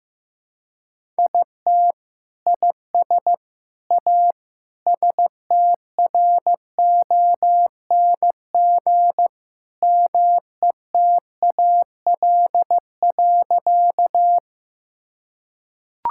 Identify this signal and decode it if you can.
Morse code, heard as a keyed tone, about 15 wpm, 700 Hz. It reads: IT IS A STRONG METAL.
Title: Sentences from Top 1000 Words - Review Part 6 - 15wpm